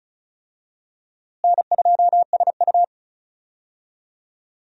Morse code, heard as a keyed tone, about 36 wpm, 700 Hz. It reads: N2SU
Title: USA Call Signs - 4 Characters - Repeated 3x - 35wpm